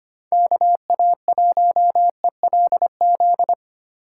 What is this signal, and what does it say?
Morse code, heard as a keyed tone, about 25 wpm, 700 Hz. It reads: KA1ELZ
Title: USA Call Signs - 6 Characters - 2x Character Spacing - 25wpm